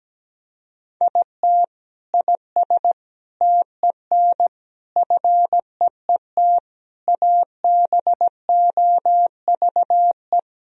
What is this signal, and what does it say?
Morse code, heard as a keyed tone, about 17 wpm, 700 Hz. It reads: IT IS TEN FEET ABOVE
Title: Sentences from Top 1000 Words - Review Part 4 - 17wpm